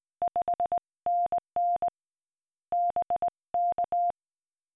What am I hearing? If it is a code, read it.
Morse code, heard as a keyed tone, about 20 wpm, 700 Hz. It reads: 5NN BK